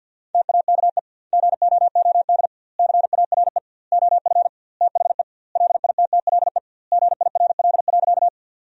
Morse code, heard as a keyed tone, about 50 wpm, 700 Hz. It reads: TAKE GOOD CARE OF THE LITTLE GIRL.
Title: 2x Word-Spacing - Sentences from Top 400 Words - 50wpm